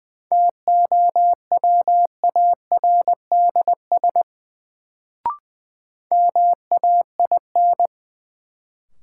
Morse code, heard as a keyed tone, about 20 wpm, 700 Hz. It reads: TOWARDS MAIN
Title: Sentences from Top 1000 Words - Review Part 2 - New Word Drill - 20wpm